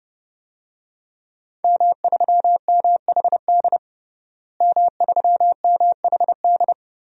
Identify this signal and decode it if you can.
Morse code, heard as a keyed tone, about 32 wpm, 700 Hz. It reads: M3MHD M3MHD
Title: DXCC-100 Call Signs - 5 Characters - Repeated 3x - 30wpm